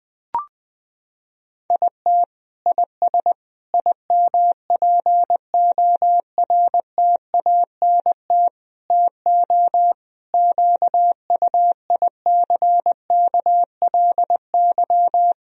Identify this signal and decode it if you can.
Morse code, heard as a keyed tone, about 20 wpm, 700 Hz. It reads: IT IS IMPORTANT TO QUICKLY
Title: Sentences from Top 1000 Words - Review Part 4 - 20wpm